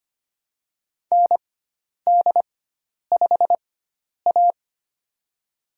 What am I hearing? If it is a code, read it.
Morse code, heard as a keyed tone, about 25 wpm, spaced as for 10 wpm, 700 Hz. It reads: ND5A
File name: Lesson 16g - Letter c - Call Signs 25wpm